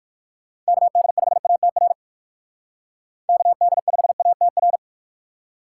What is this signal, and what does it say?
Morse code, heard as a keyed tone, about 64 wpm, spaced as for 56 wpm, 700 Hz. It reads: KD5ATR KD5ATR
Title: Speed Racing - USA 2x3 Call Signs 45wpm